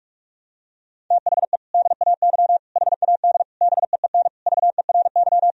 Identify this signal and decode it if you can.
Morse code, heard as a keyed tone, about 42 wpm, 700 Hz. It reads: THE DAY HAD BEEN VERY
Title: Sentences from Top 1000 Words - Review Part 2 - 45wpm